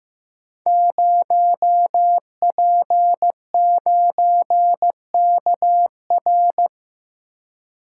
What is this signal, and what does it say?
Morse code, heard as a keyed tone, about 15 wpm, 700 Hz. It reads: 0P9KR
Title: Code Groups - 5 Letter:Numbers - 15wpm